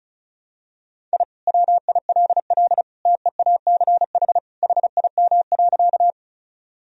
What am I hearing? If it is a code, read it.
Morse code, heard as a keyed tone, about 34 wpm, 700 Hz. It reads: I WILL TEACH HIM.
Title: Sentences from Top 1000 Words - Review Part 2 - 35wpm